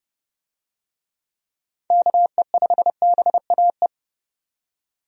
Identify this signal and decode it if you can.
Morse code, heard as a keyed tone, about 30 wpm, 700 Hz. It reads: KE5BAE
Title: USA Call Signs - 6 Characters - 30wpm